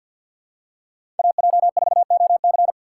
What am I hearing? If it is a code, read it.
Morse code, heard as a keyed tone, about 48 wpm, 700 Hz. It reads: AJ4OC
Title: USA Call Signs - 5 Characters - 3x Character Spacing - 50wpm